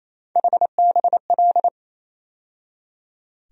Morse code, heard as a keyed tone, about 28 wpm, 700 Hz. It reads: HBL
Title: USA Call Sign Postfixes - 3 Letters 28wpm